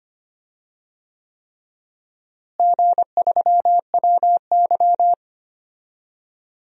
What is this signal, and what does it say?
Morse code, heard as a keyed tone, about 25 wpm, 700 Hz. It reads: G3WY
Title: DXCC-100 Call Signs - 4 Characters - Repeated 3x - 25wpm